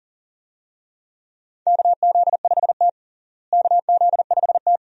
Morse code, heard as a keyed tone, about 37 wpm, 700 Hz. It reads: KZ5T KZ5T
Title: USA Call Signs - 4 Characters - Repeated 2x - 40wpm